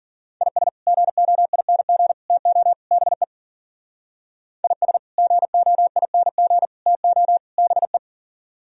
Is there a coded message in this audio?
Yes: Morse code from 0.4 to 8.0 s, heard as a keyed tone, about 50 wpm, 700 Hz. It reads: IS GOING TO BE IS GOING TO BE